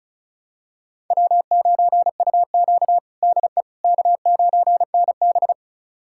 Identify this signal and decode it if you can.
Morse code, heard as a keyed tone, about 36 wpm, 700 Hz. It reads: W9UQ DE K9NB
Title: USA Callsign DE Callsign - 4 Characters - 35wpm